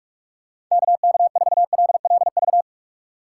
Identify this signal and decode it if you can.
Morse code, heard as a keyed tone, about 44 wpm, 700 Hz. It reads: KK4LLV
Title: USA Call Signs - 6 Characters - 4x Character Spacing - 45wpm